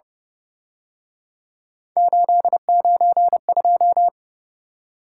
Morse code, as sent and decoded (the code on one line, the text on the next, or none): ---.. ----. ..---
892